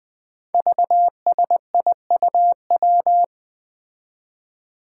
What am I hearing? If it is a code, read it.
Morse code, heard as a keyed tone, about 21 wpm, 700 Hz. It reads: VSIUW
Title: Five Challenging Letter-Number Combinations 20wpm